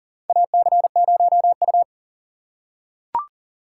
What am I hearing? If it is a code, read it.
Morse code, heard as a keyed tone, about 40 wpm, 700 Hz. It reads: AC0U
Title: USA Call Signs - 4 Characters - 4x Character Spacing - 40wpm